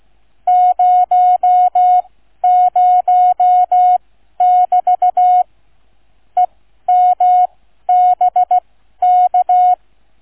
Morse code, heard as a keyed tone, about 16 wpm, spaced as for 12 wpm, 730 Hz. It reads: 00<BT> EMBK
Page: QRS International Flight Contest 2025 Demo Flight NR 00 Sound files Listen-to or download incoming QNC sound file